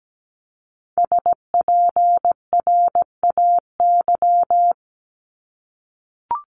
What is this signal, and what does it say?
Morse code, heard as a keyed tone, about 17 wpm, 700 Hz. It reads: SPRAY
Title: Multi-letter Phonemes 17wpm